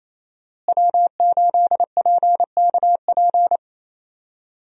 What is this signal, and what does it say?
Morse code, heard as a keyed tone, about 28 wpm, 700 Hz. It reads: W8PKP
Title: USA Call Signs - 5 Characters - Repeated 2x - 28wpm